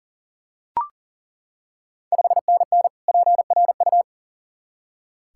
Morse code, heard as a keyed tone, about 40 wpm, 700 Hz. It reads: 5NN PRU
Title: Ontario QSO Party Exchange 40wpm